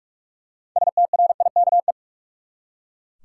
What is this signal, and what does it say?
Morse code, heard as a keyed tone, about 44 wpm, 700 Hz. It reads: STRIKE
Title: Six Letter Words - 45wpm